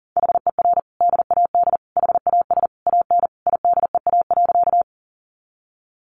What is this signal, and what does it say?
Morse code, heard as a keyed tone, about 40 wpm, 700 Hz. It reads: HER DAD HAS AN IDEA.
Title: Easy Phrases 40wpm